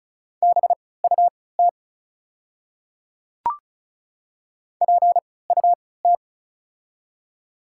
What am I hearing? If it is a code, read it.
Morse code, heard as a keyed tone, about 34 wpm, spaced as for 20 wpm, 690 Hz. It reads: BUT PUT
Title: 3x Character Spacing - Top 500 Words - 35wpm